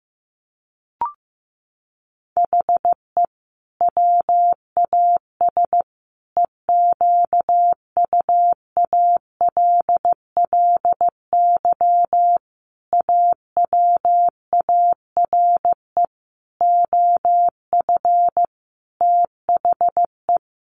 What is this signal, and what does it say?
Morse code, heard as a keyed tone, about 15 wpm, 700 Hz. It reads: HE WAS EQUALLY AWARE OF THE
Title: Sentences from Top 1400 Words 15wpm